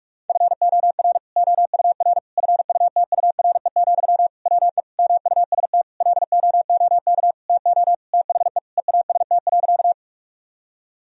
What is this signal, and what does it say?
Morse code, heard as a keyed tone, about 41 wpm, 700 Hz. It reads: FOR OUR FUTURE, WE MUST LOOK TO THE EAST.
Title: Sentences from Top 1000 Words - Review Part 3 - 45wpm